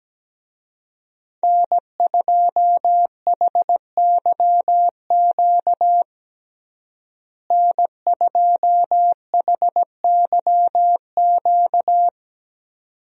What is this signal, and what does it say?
Morse code, heard as a keyed tone, about 17 wpm, 700 Hz. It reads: N2HYQ N2HYQ
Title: USA Call Signs - 5 Characters - Repeated 2x - 17wpm